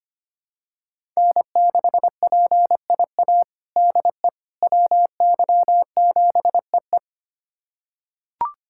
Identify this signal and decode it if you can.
Morse code, heard as a keyed tone, about 25 wpm, 700 Hz. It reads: N6PIA DE WY7EE
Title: USA Callsign DE Callsign - 5 Characters - 25wpm